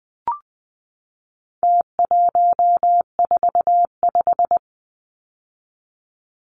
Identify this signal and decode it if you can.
Morse code, heard as a keyed tone, about 20 wpm, 700 Hz. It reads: T145
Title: Numbers - Contest Style - 4-Digit 20wpm